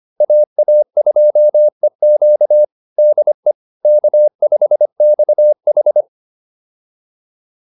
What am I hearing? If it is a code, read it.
Morse code, heard as a keyed tone, about 25 wpm, 600 Hz. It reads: AA2EQ DE K5XH
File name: QSO210a 25